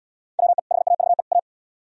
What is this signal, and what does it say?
Morse code, heard as a keyed tone, about 77 wpm, 700 Hz. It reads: WE HAVE A